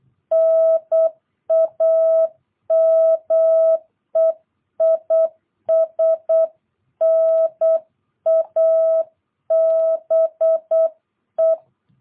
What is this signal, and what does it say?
Morse code, heard as a keyed tone, about 8 wpm, 650 Hz. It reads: NAME IS NABE
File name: jh8cbh / 2022cw